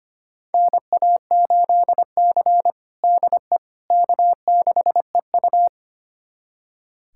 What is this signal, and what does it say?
Morse code, heard as a keyed tone, about 25 wpm, 700 Hz. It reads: NA8C DE K6EU
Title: USA Callsign DE Callsign - 4 Characters - 25wpm